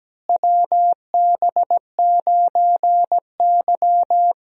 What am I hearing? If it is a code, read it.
Morse code, heard as a keyed tone, about 17 wpm, 700 Hz. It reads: WB9Y
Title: USA Call Signs - 4 Characters - Repeated 3x - 17wpm